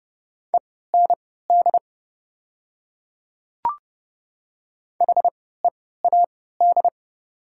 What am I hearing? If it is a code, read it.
Morse code, heard as a keyed tone, about 29 wpm, spaced as for 17 wpm, 700 Hz. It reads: END HEAD
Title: 3x Character Spacing - Top 500 Words - 30wpm